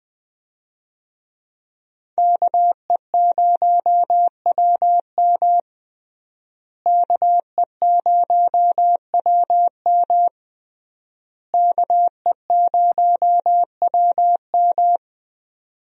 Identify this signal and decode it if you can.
Morse code, heard as a keyed tone, about 21 wpm, 700 Hz. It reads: KE0WM KE0WM KE0WM
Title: USA Call Signs - 5 Characters - Repeated 3x - 20wpm